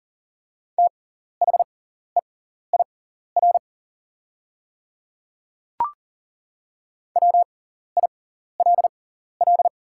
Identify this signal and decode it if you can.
Morse code, heard as a keyed tone, about 40 wpm, spaced as for 14 wpm, 700 Hz. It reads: THEIR WILL
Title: 6x Character Spacing - Top 100 Words - 40wpm